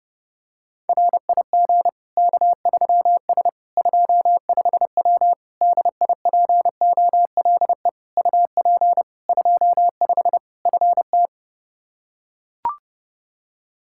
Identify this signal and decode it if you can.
Morse code, heard as a keyed tone, about 30 wpm, 700 Hz. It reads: RIG K3S 25W DIPOLE UP 25 FT
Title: K1USN SST QSO Practice 30wpm